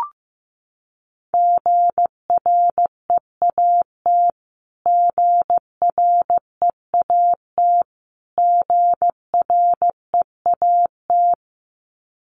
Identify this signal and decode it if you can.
Morse code, heard as a keyed tone, about 14 wpm, 700 Hz. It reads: GREAT GREAT GREAT
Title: Top 200 Words Repeated 15wpm